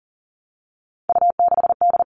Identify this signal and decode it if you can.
Morse code, heard as a keyed tone, about 40 wpm, 700 Hz. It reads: U6D
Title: All Cut Numbers - 3-Digit 40wpm